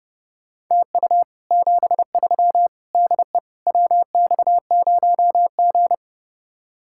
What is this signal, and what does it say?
Morse code, heard as a keyed tone, about 37 wpm, 700 Hz. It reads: TU 73 DE WX0G
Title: Speed-Racing Common QSO Phrases - 30wpm